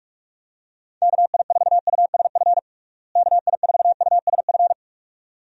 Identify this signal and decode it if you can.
Morse code, heard as a keyed tone, about 42 wpm, 700 Hz. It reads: KI4USF KI4USF
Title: USA Call Signs - 6 Characters - Repeated 2x - 45wpm